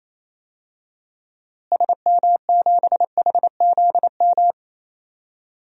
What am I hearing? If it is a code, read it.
Morse code, heard as a keyed tone, about 28 wpm, 700 Hz. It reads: SM7HZM
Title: DXCC-100 Call Signs - 6 Characters - Repeated 3x - 28wpm